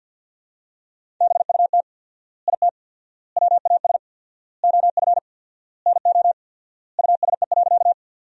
Morse code, heard as a keyed tone, about 48 wpm, 700 Hz. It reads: BUT IT WAS OF NO USE.
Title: 4x Word-Spacing - Sentences from Top 100 Words - 50wpm